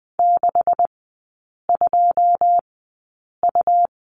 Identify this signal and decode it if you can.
Morse code, heard as a keyed tone, about 20 wpm, spaced as for 8 wpm, 700 Hz. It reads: 62U
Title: 127 Lesson 21e - Number 6 - Three Characters 20wpm